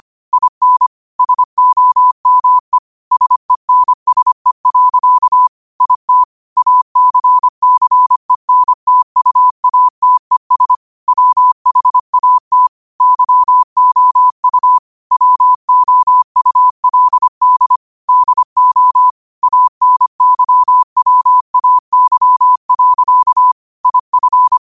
Morse code, heard as a keyed tone, about 25 wpm, 1000 Hz. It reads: IN SOME SENSE. IT ACCENTUATES WHAT YOU WOULD DO ANYWAY. IF